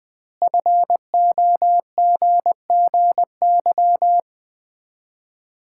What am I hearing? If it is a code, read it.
Morse code, heard as a keyed tone, about 21 wpm, 700 Hz. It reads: FOGGY
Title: Common QSO Phrases - 2x Word-Spacing - 20wpm